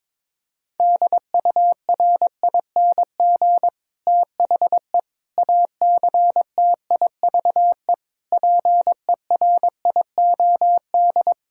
Morse code, heard as a keyed tone, about 22 wpm, 700 Hz. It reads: DURING THE ACTIVE PERIOD
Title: Sentences from Top 1000 Words - Review Part 6 - 22wpm